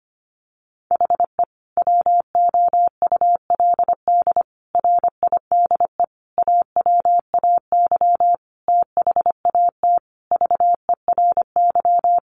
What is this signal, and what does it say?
Morse code, heard as a keyed tone, about 25 wpm, 700 Hz. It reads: HE WOULD RIDE AWAY THAT VERY
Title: Sentences from Top 1200 Words 25wpm